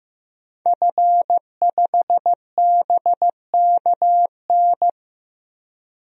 Morse code, heard as a keyed tone, about 16 wpm, 700 Hz. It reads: F5BKN
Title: DXCC-100 Call Signs - 5 Characters - 15wpm